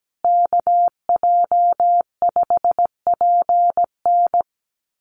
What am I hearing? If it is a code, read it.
Morse code, heard as a keyed tone, about 17 wpm, 700 Hz. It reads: KJ5PN
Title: USA Call Signs Repeated 2x 17wpm